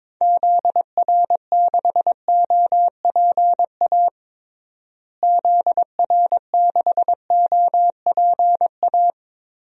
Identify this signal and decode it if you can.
Morse code, heard as a keyed tone, about 22 wpm, 700 Hz. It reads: ZR6OPA ZR6OPA
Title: DXCC-100 Call Signs - 6 Characters - Repeated 2x - 22wpm